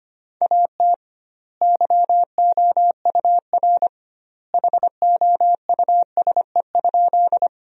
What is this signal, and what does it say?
Morse code, heard as a keyed tone, about 25 wpm, 700 Hz. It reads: AT YOUR HOUSE?
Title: 2x Word-Spacing - Sentences from Top 200 Words - 25wpm